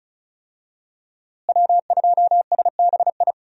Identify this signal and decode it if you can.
Morse code, heard as a keyed tone, about 34 wpm, 700 Hz. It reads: W2SBI
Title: USA Call Signs - 5 Characters - Repeated 3x - 35wpm